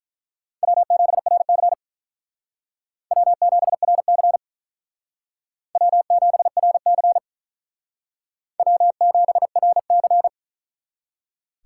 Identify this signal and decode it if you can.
Morse code, heard as a keyed tone, about 42 wpm, 700 Hz. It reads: W7RC W7RC W7RC W7RC
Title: Speed Racing - USA 1x2 Call Signs 35wpm